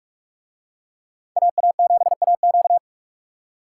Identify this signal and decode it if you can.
Morse code, heard as a keyed tone, about 48 wpm, 700 Hz. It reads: AA7AQ